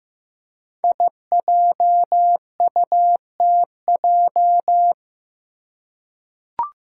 Morse code, heard as a keyed tone, about 15 wpm, 700 Hz. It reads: IJUTJ
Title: Five Challenging Letter-Number Combinations 15wpm